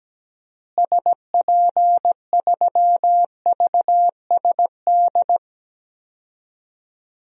Morse code, heard as a keyed tone, about 17 wpm, 700 Hz. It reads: SP3VSD
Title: DXCC-100 Call Signs - 6 Characters - Repeated 3x - 17wpm